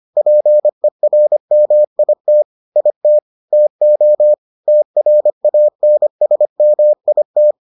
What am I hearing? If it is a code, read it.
Morse code, heard as a keyed tone, about 25 wpm, 600 Hz. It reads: PERMIT IT TO TRANSMIT